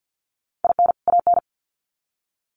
Morse code, heard as a keyed tone, about 50 wpm, 700 Hz. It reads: IN AN
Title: Sets of 2 Words 50wpm